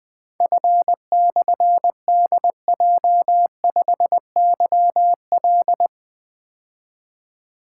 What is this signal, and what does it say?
Morse code, heard as a keyed tone, about 21 wpm, 700 Hz. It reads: F/DJ5YL
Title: Difficult Call Signs 20wpm